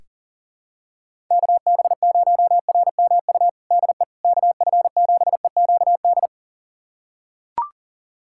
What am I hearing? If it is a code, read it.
Morse code, heard as a keyed tone, about 40 wpm, 700 Hz. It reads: KB0RMU DE KF7EQD
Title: USA Callsign DE Callsign - 6 Characters - 40wpm